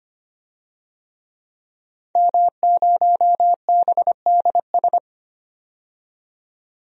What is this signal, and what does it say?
Morse code, heard as a keyed tone, about 25 wpm, 700 Hz. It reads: M0BDS
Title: DXCC-100 Call Signs - 5 Characters - 25wpm